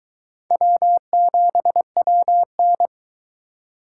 Morse code, heard as a keyed tone, about 22 wpm, 700 Hz. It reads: W7WN